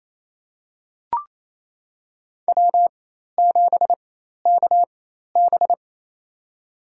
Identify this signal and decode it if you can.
Morse code, heard as a keyed tone, about 28 wpm, spaced as for 13 wpm, 700 Hz. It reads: W7KB
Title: USA Call Signs - 4 Characters - 4x Character Spacing - 28wpm